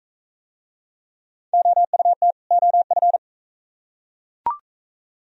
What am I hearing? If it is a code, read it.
Morse code, heard as a keyed tone, about 38 wpm, 700 Hz. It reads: OUT OF